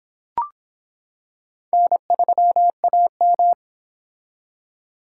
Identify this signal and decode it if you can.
Morse code, heard as a keyed tone, about 25 wpm, 700 Hz. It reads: N3AM